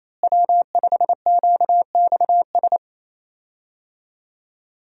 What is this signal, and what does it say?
Morse code, heard as a keyed tone, about 28 wpm, 700 Hz. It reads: W5QXS
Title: USA Call Signs - 5 Characters - Repeated 3x - 28wpm